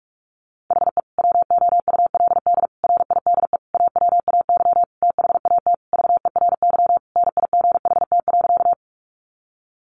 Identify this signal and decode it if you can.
Morse code, heard as a keyed tone, about 44 wpm, 700 Hz. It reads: HE WOULD RIDE AWAY THAT VERY NIGHT.
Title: Sentences from Top 1200 Words 45wpm